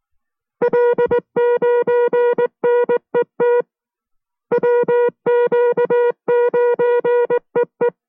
Morse code, heard as a keyed tone, about 19 wpm, 480 Hz. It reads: L9NET WQ9EE